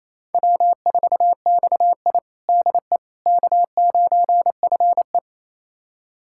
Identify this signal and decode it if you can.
Morse code, heard as a keyed tone, about 28 wpm, 700 Hz. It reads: W4XI DE K9FE
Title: USA Callsign DE Callsign - 4 Characters - 28wpm